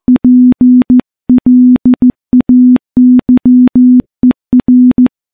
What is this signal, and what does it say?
Morse code, heard as a keyed tone, about 15 wpm, 250 Hz. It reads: PLAYER
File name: morse-training-15WPM